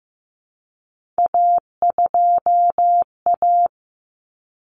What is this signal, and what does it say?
Morse code, heard as a keyed tone, about 15 wpm, 700 Hz. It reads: A2A
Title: Common Cut Numbers - 3-Digit 15wpm